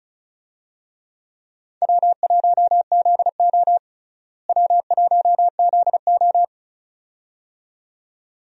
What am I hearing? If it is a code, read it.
Morse code, heard as a keyed tone, about 36 wpm, 700 Hz. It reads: W1ZO W1ZO
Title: USA Call Signs - 4 Characters - Repeated 2x - 35wpm